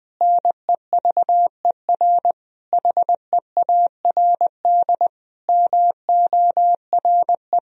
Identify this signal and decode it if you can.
Morse code, heard as a keyed tone, about 20 wpm, 700 Hz. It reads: NEVER HEARD MORE
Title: Sentences from Top 1000 Words - Review Part 1 - 20wpm